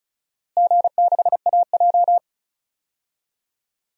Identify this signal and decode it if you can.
Morse code, heard as a keyed tone, about 36 wpm, 700 Hz. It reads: G6AJ
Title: DXCC-100 Call Signs - 4 Characters - Repeated 2x - 35wpm